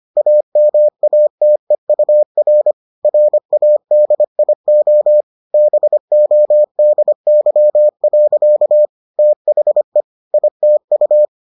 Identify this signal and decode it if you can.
Morse code, heard as a keyed tone, about 25 wpm, 600 Hz. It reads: AMATEUR RADIO BODY. THE ITU